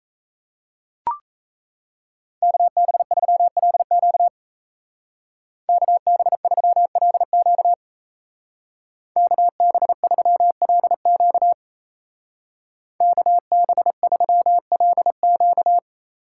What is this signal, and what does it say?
Morse code, heard as a keyed tone, about 37 wpm, 700 Hz. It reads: KB3LQ KB3LQ KB3LQ KB3LQ